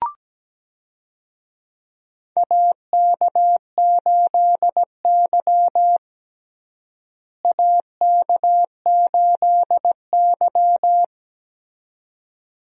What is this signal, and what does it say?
Morse code, heard as a keyed tone, about 17 wpm, 700 Hz. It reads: AK8Y AK8Y
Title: USA Call Signs - 4 Characters - Repeated 2x - 17wpm